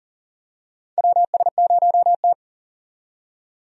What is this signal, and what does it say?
Morse code, heard as a keyed tone, about 38 wpm, 700 Hz. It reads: WS0T
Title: USA Call Signs - 4 Characters - 40wpm